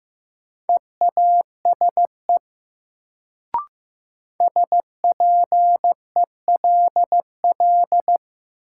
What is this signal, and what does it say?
Morse code, heard as a keyed tone, about 15 wpm, 700 Hz. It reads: EASE SPELL
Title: Sentences from Top 1000 Words - Review Part 1 - New Word Drill - 15wpm